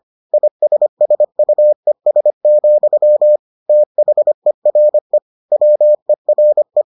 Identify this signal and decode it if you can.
Morse code, heard as a keyed tone, about 25 wpm, 600 Hz. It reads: ISSUES, THERE WERE